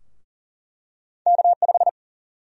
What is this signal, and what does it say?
Morse code, heard as a keyed tone, about 40 wpm, 700 Hz. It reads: K5
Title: USA Call Sign Prefixes - 40wpm